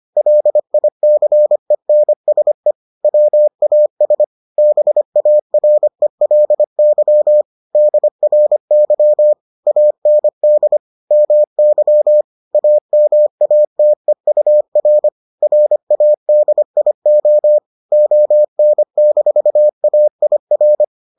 Morse code, heard as a keyed tone, about 25 wpm, 600 Hz. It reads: LICENSE WAS BARELY DRY AND MY AMATEUR RADIO ON-AIR